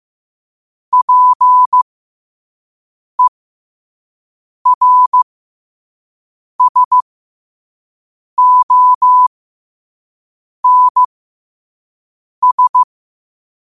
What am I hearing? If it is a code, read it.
Morse code, heard as a keyed tone, about 15 wpm, spaced as for 5 wpm, 1000 Hz. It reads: PERSONS
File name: qotdfiles / 5wpm